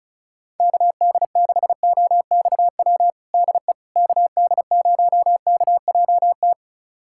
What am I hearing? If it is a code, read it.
Morse code, heard as a keyed tone, about 36 wpm, 700 Hz. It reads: KD6OXW DE KD0KJT
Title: USA Callsign DE Callsign - 6 Characters - 35wpm